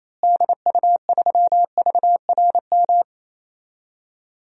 Morse code, heard as a keyed tone, about 28 wpm, 700 Hz. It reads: DU3VRM
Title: DXCC-100 Call Signs - 6 Characters - Repeated 3x - 28wpm